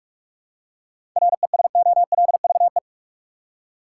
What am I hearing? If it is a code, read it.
Morse code, heard as a keyed tone, about 46 wpm, 700 Hz. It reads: RESOLVE
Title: Seven Letter Words - 45wpm